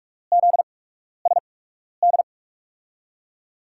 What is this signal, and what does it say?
Morse code, heard as a keyed tone, about 45 wpm, spaced as for 12 wpm, 700 Hz. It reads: ZSD
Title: USA Call Signs - 5 Characters - 8x Character Spacing - 45wpm